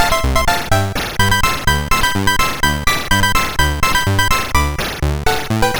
There is no code.